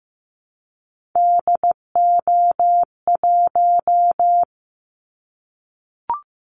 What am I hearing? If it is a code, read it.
Morse code, heard as a keyed tone, about 15 wpm, 700 Hz. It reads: DO1
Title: Callsign Prefixes for Top 25 DXCC - 15wpm